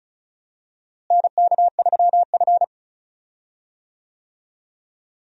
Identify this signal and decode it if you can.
Morse code, heard as a keyed tone, about 36 wpm, 700 Hz. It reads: NK3F
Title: USA Call Signs - 4 Characters - Repeated 3x - 35wpm